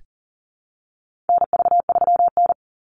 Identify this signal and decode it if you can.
Morse code, heard as a keyed tone, about 40 wpm, 700 Hz. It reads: NV3N
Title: Call Signs Repeated 2x - 40wpm